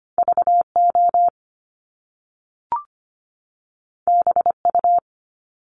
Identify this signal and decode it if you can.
Morse code, heard as a keyed tone, about 25 wpm, 700 Hz. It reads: VO BU
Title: Top Two Letter Combinations in Words 25wpm